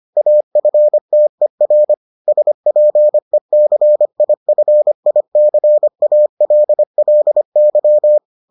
Morse code, heard as a keyed tone, about 25 wpm, 600 Hz. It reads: AFTER SPECIFICALLY